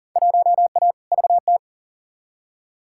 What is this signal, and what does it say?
Morse code, heard as a keyed tone, about 40 wpm, 700 Hz. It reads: 1A VT
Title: Field Day Exchanges Encore - 40wpm